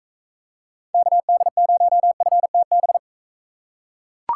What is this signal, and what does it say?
Morse code, heard as a keyed tone, about 44 wpm, 700 Hz. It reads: KD0FTB